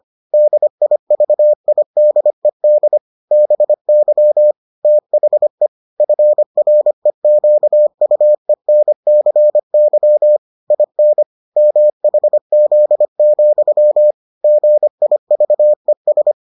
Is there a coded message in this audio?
Yes: Morse code from 0.3 to 16.3 s, heard as a keyed tone, about 25 wpm, 600 Hz. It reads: DIVIDED BY THE FREQUENCY IN MHZ, GIVES